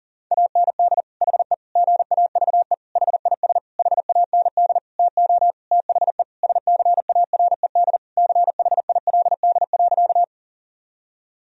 Morse code, heard as a keyed tone, about 40 wpm, 700 Hz. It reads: AND HE GAVE HIS HAND TO THE SCARED CHILD.
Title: Sentences from Top 1000 Words - Review Part 4 - 40wpm